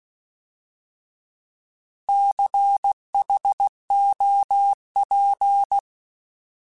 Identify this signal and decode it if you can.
Morse code, heard as a keyed tone, about 16 wpm, 800 Hz. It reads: CHOP
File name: Alphabetical / Alphabetical mp3s / Cs @15